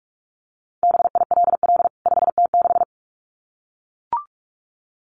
Morse code, heard as a keyed tone, about 44 wpm, 700 Hz. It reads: BILL 5T6
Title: CWT Exchange 45wpm